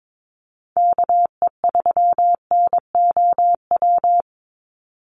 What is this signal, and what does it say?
Morse code, heard as a keyed tone, about 22 wpm, 700 Hz. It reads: KE3NOW
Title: USA Call Signs 22wpm